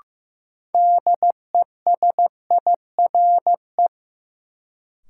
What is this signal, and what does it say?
Morse code, heard as a keyed tone, about 15 wpm, 700 Hz. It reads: DESIRE
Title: Sentences from Top 1100 Words - New Word Drill - 15wpm